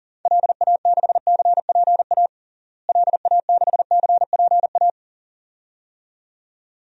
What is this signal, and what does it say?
Morse code, heard as a keyed tone, about 40 wpm, 700 Hz. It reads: LA6CPA LA6CPA
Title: DXCC-100 Call Signs - 6 Characters - Repeated 2x - 40wpm